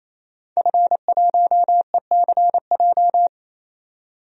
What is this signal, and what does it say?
Morse code, heard as a keyed tone, about 28 wpm, 700 Hz. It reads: F1ECJ
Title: DXCC-100 Call Signs - 5 Characters - 28wpm